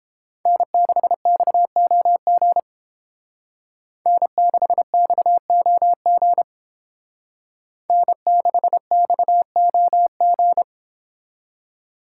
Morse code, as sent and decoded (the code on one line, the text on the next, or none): -. -.... -..- --- --. / -. -.... -..- --- --. / -. -.... -..- --- --.
N6XOG N6XOG N6XOG